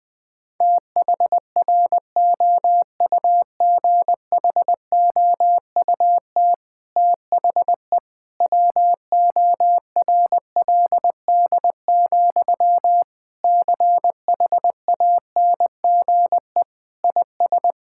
Morse code, heard as a keyed tone, about 21 wpm, 700 Hz. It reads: THROUGHOUT THE WORLD, CHANGE IS
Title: Sentences from Top 1000 Words - Review Part 5 - 20wpm